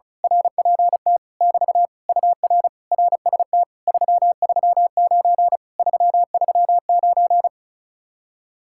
Morse code, heard as a keyed tone, about 32 wpm, 700 Hz. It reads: RPT <BT> UR RST 339 339
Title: Speed-Racing Common QSO Phrases - 35wpm